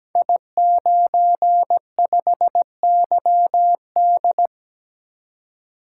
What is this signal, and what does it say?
Morse code, heard as a keyed tone, about 17 wpm, 700 Hz. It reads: I95YD
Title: Code Groups - 5 Letter:Numbers - 17wpm